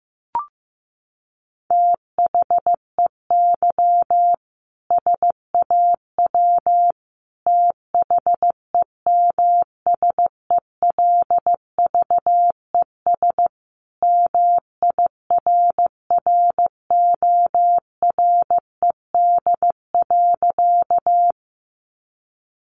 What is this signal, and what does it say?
Morse code, heard as a keyed tone, about 15 wpm, 700 Hz. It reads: THEY SAW THEMSELVES MIRRORED.
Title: A Sound Of Thunder 15wpm